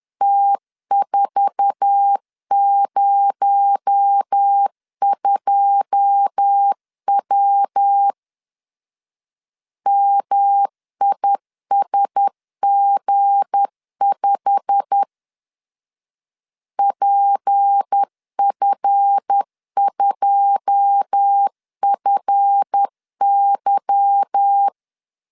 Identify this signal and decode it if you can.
Morse code, heard as a keyed tone, about 10 wpm, 800 Hz. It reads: T402W MISG5 PF2FY
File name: LEZIONI / 9 2 10wpm 2736